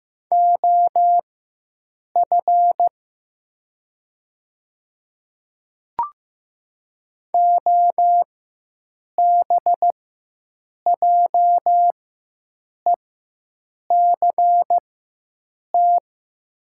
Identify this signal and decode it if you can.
Morse code, heard as a keyed tone, about 15 wpm, spaced as for 7 wpm, 700 Hz. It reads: OF OBJECT
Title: 4x Character Spacing - Top 500 Words - 15wpm